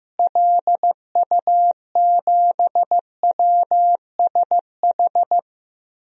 Morse code, heard as a keyed tone, about 15 wpm, 700 Hz. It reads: LU7WSH
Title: DXCC-100 Call Signs - 6 Characters - Repeated 3x - 15wpm